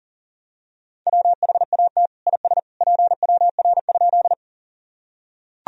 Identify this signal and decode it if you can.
Morse code, heard as a keyed tone, about 40 wpm, 700 Hz. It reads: WHAT IS PWR?
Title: Sentences from Top 500 Words and Abbreviations 40wpm